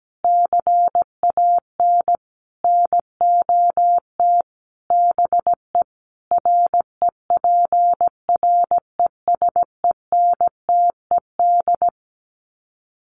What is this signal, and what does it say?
Morse code, heard as a keyed tone, about 17 wpm, 700 Hz. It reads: CAN NOT BE REPRESENTED
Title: Sets of 5 Words 17wpm